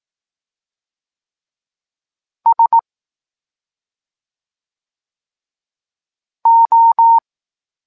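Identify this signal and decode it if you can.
Morse code, heard as a keyed tone, about 18 wpm, 920 Hz. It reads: S O
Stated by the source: Evenly spaced Morse lesson text 21 for learning letters G and Z, from SP9MOA Niepołomicki Klub Krótkofalowców.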